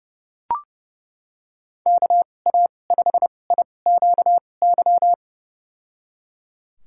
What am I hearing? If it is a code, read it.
Morse code, heard as a keyed tone, about 29 wpm, spaced as for 21 wpm, 700 Hz. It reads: KA5IQY
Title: USA Call Signs - 2x Character Spacing - 30wpm